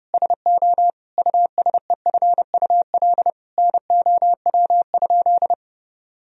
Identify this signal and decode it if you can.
Morse code, heard as a keyed tone, about 32 wpm, 700 Hz. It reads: SO USEFUL NOW?
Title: Sentences from Top 1000 Words - Review Part 4 - 30wpm